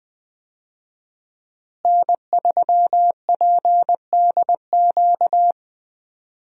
Morse code, heard as a keyed tone, about 21 wpm, 700 Hz. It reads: N3PDQ
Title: USA Call Signs - 5 Characters - Repeated 2x - 20wpm